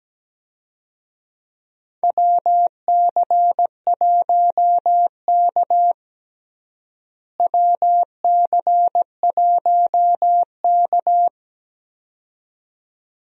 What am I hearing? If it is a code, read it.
Morse code, heard as a keyed tone, about 17 wpm, 700 Hz. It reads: WC1K WC1K
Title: USA Call Signs - 4 Characters - Repeated 2x - 17wpm